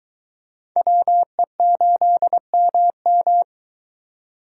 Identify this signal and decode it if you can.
Morse code, heard as a keyed tone, about 23 wpm, 700 Hz. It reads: WE8MM